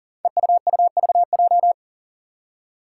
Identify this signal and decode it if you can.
Morse code, heard as a keyed tone, about 40 wpm, 700 Hz. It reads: EUUVJ
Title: Five Challenging Letter-Number Combinations 40wpm